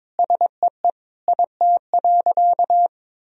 Morse code, heard as a keyed tone, about 21 wpm, 700 Hz. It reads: SEE IT.
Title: Sentences from Top 1000 Words - Review Part 4 - 22wpm